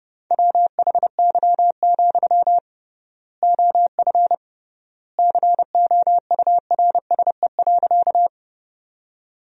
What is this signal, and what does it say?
Morse code, heard as a keyed tone, about 30 wpm, 700 Hz. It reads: WHY, OF COURSE.
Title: 3x Word-Spacing - Sentences from Top 500 Words - 30wpm